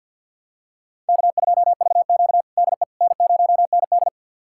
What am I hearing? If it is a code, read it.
Morse code, heard as a keyed tone, about 52 wpm, 700 Hz. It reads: K2VQ DE N0ND
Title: USA Callsign DE Callsign - 4 Characters - 50wpm